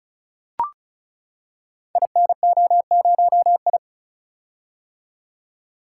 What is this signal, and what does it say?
Morse code, heard as a keyed tone, about 36 wpm, 700 Hz. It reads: INO0I
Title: Code Groups - 5 Letter:Numbers - 35wpm